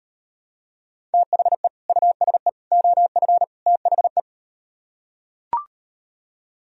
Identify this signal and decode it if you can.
Morse code, heard as a keyed tone, about 38 wpm, 700 Hz. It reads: THE USE OF THE